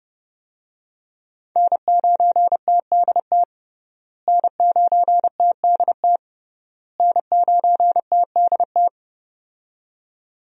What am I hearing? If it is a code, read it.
Morse code, heard as a keyed tone, about 30 wpm, 700 Hz. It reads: N9TDT N9TDT N9TDT
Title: USA Call Signs - 5 Characters - Repeated 3x - 30wpm